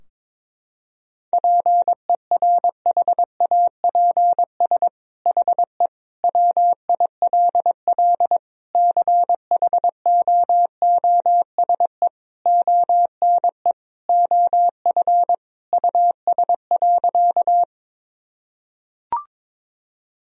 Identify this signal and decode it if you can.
Morse code, heard as a keyed tone, about 22 wpm, 700 Hz. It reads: PERHAPS HE WILL CHOOSE ONE OF US.
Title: Sentences from Top 1000 Words - Review Part 4 - 22wpm